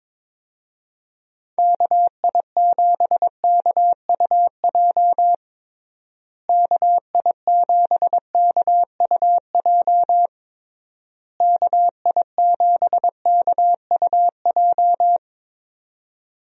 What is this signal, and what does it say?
Morse code, heard as a keyed tone, about 22 wpm, 700 Hz. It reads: KI7KUJ KI7KUJ KI7KUJ
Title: USA Call Signs - 6 Characters - Repeated 3x - 22wpm